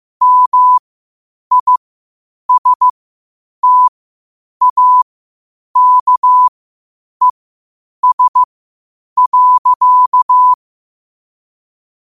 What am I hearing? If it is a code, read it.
Morse code, heard as a keyed tone, about 15 wpm, spaced as for 8 wpm, 1000 Hz. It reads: MISTAKES.
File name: qotdfiles / 8wpm